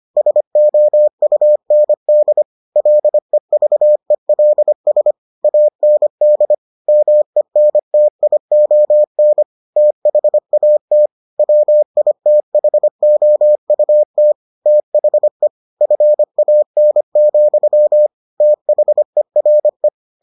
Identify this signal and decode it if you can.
Morse code, heard as a keyed tone, about 25 wpm, 600 Hz. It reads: SOUND LEVELS AND MENTION THAT WITHOUT THE FAN, THERE